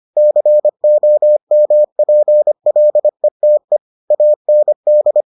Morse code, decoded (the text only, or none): COMPLETE AND